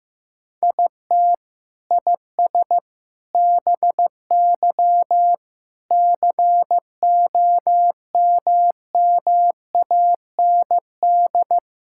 Morse code, heard as a keyed tone, about 15 wpm, 700 Hz. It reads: IT IS BY COMMAND
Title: Sentences from Top 1600 Words 15wpm